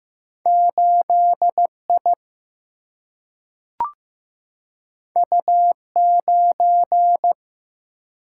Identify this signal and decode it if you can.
Morse code, heard as a keyed tone, about 15 wpm, 700 Hz. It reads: 8I U9
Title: Single-Letter Single-Number 15wpm